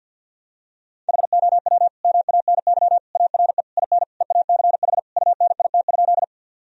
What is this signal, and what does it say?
Morse code, heard as a keyed tone, about 50 wpm, 700 Hz. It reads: HOW MANY ARE IN EACH UNIT?
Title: Sentences from Top 1000 Words - Review Part 2 - 50wpm